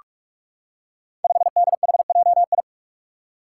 Morse code, heard as a keyed tone, about 44 wpm, 700 Hz. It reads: 5DHJI
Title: Five Challenging Letter-Number Combinations 45wpm